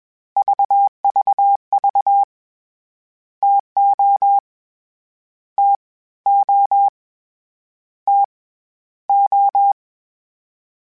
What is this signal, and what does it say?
Morse code, heard as a keyed tone, about 22 wpm, 800 Hz. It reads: VVV TO T O T O
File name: Variable Spacing / Variable Spacing-mp3s / TOs @20